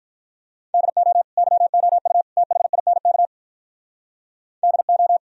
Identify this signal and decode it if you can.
Morse code, heard as a keyed tone, about 45 wpm, 700 Hz. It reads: DO YOU THINK DO